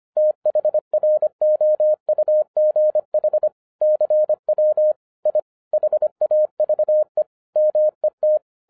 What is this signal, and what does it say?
Morse code, heard as a keyed tone, about 25 wpm, 610 Hz. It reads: THROUGH CW I HAVE MET